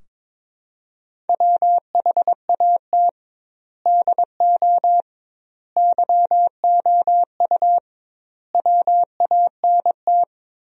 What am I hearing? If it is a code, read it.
Morse code, heard as a keyed tone, about 21 wpm, 700 Hz. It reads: WHAT DO YOU WANT
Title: 2x Word-Spacing - Sentences from Top 200 Words - 22wpm